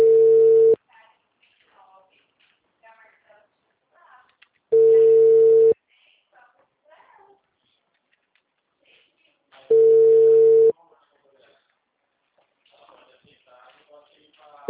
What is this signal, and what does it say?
Ringback tone (China, 450 Hz), with speech.